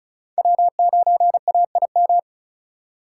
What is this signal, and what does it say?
Morse code, heard as a keyed tone, about 46 wpm, 700 Hz. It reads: W9AIM